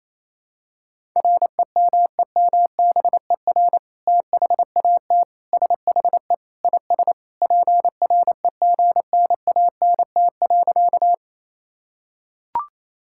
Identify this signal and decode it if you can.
Morse code, heard as a keyed tone, about 28 wpm, 700 Hz. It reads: REMEMBER THAT SHE IS PREGNANT.
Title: Sentences from Top 1000 Words - Review Part 6 - 28wpm